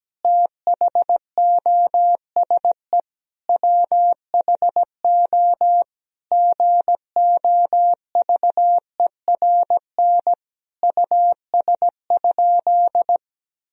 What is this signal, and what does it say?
Morse code, heard as a keyed tone, about 17 wpm, 700 Hz. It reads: THOSE WHO GOVERN US?
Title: Sentences from Top 1000 Words - Review Part 2 - 17wpm